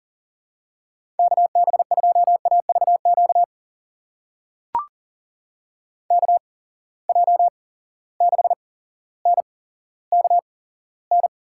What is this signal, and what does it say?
Morse code, heard as a keyed tone, about 40 wpm, spaced as for 17 wpm, 700 Hz. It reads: KB2AVQ KJ6NKN
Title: USA Call Signs - 6 Characters - 8x Character Spacing - 40wpm